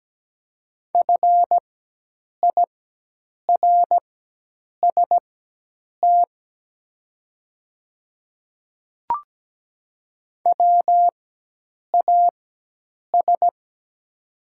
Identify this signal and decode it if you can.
Morse code, heard as a keyed tone, about 17 wpm, spaced as for 8 wpm, 700 Hz. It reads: FIRST WAS
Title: 4x Character Spacing - Top 100 Words - 17wpm